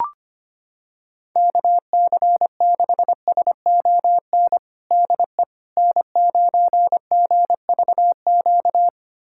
Morse code, heard as a keyed tone, about 25 wpm, 700 Hz. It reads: KC6SON DE N9GVQ
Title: USA Callsign DE Callsign 25wpm